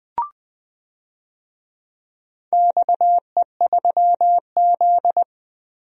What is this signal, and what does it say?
Morse code, heard as a keyed tone, about 20 wpm, 700 Hz. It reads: XE3Z
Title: DXCC-100 Call Signs - 4 Characters - Repeated 2x - 20wpm